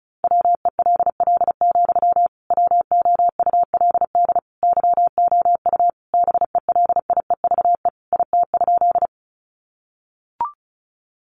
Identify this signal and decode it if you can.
Morse code, heard as a keyed tone, about 34 wpm, 700 Hz. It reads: WELL, WOULD YOU BELIEVE IT?
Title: Sentences from Top 1000 Words 35wpm